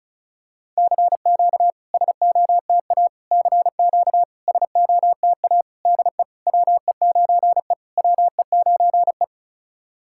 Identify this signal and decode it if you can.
Morse code, heard as a keyed tone, about 36 wpm, 700 Hz. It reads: CQ SOTA CQ SOTA DE WE9E WE9E
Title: SOTA QSO Practice 35wpm